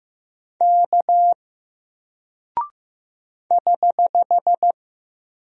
Morse code, heard as a keyed tone, about 15 wpm, 700 Hz. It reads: K <HH>
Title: Single Character (Letter:Number:Punctuation:Prosign) 15wpm